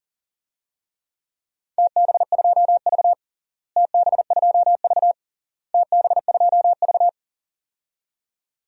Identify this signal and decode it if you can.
Morse code, heard as a keyed tone, about 37 wpm, 700 Hz. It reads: TB2V TB2V TB2V
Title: DXCC-100 Call Signs - 4 Characters - Repeated 3x - 40wpm